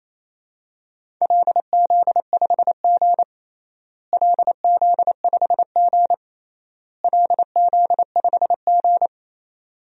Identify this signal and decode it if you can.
Morse code, heard as a keyed tone, about 28 wpm, 700 Hz. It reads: LZ5G LZ5G LZ5G
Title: DXCC-100 Call Signs - 4 Characters - Repeated 3x - 28wpm